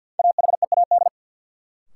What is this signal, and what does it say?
Morse code, heard as a keyed tone, about 53 wpm, 700 Hz. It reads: AHEAD
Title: Sentences from Top 1000 Words - Review Part 6 - New Word Drill - 50wpm